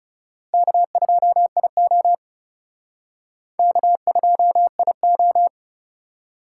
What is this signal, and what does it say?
Morse code, heard as a keyed tone, about 38 wpm, 700 Hz. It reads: K2IO K2IO
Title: Speed Racing - USA 1x2 Call Signs 30wpm